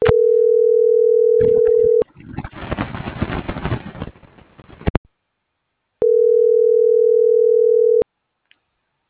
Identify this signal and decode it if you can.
Ringback tone (North America), 440 + 480 Hz.
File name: CENTRAL GENERAL / out tania.lemus